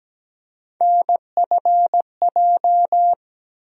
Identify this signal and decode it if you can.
Morse code, heard as a keyed tone, about 17 wpm, 700 Hz. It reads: NFJ
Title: USA Call Sign Postfixes - 3 Letters 17wpm